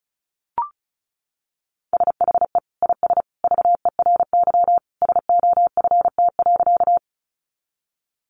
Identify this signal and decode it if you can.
Morse code, heard as a keyed tone, about 34 wpm, 700 Hz. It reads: SHE IS VERY SOFT.
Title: Easy Phrases 35wpm